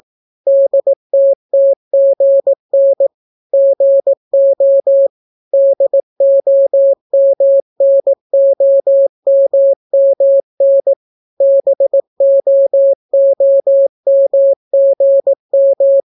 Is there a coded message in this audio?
Yes: Morse code from 0.5 to 16.0 s, heard as a keyed tone, about 18 wpm, 550 Hz. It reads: DTTGN GO DOMNOMMN BOOMGM